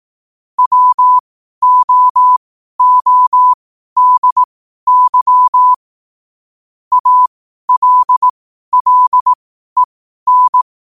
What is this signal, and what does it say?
Morse code, heard as a keyed tone, about 18 wpm, spaced as for 12 wpm, 1000 Hz. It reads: WOODY ALLEN
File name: qotdfiles / 12wpm